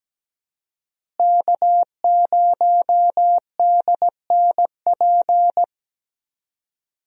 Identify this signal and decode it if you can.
Morse code, heard as a keyed tone, about 17 wpm, 700 Hz. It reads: K0DNP
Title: USA Call Signs - 5 Characters - Repeated 2x - 17wpm